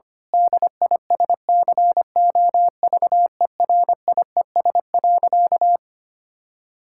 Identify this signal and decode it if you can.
Morse code, heard as a keyed tone, about 25 wpm, 700 Hz. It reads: DISCOVERIES.
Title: Pangram Challenge v2 - 25wpm